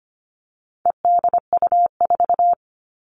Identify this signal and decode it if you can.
Morse code, heard as a keyed tone, about 25 wpm, 700 Hz. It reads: EDU4
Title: All Cut Numbers - 4-Digit 25wpm